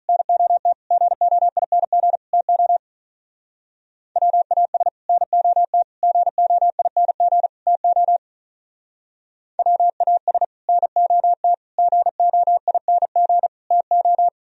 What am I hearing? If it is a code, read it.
Morse code, heard as a keyed tone, about 45 wpm, 700 Hz. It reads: NOT GOING TO WAS NOT GOING TO WAS NOT GOING TO